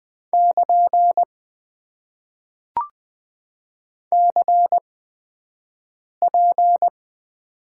Morse code, heard as a keyed tone, about 20 wpm, spaced as for 5 wpm, 700 Hz. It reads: <KN> CP
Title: Top 90 QSO Elements - 8x Character Spacing - 20wpm